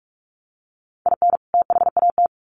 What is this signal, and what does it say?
Morse code, heard as a keyed tone, about 44 wpm, 700 Hz. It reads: IN THAT
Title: Sets of 2 Words 45wpm